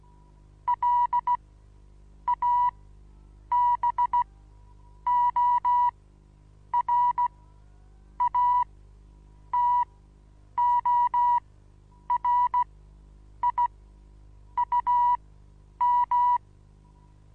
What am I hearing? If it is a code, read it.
Morse code, heard as a keyed tone, about 16 wpm, spaced as for 8 wpm, 970 Hz. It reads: LABORATORIUM
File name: Onlinelehrgang / morsen / DJ4UF Morsen25